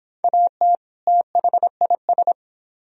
Morse code, heard as a keyed tone, about 26 wpm, 700 Hz. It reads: AT THIS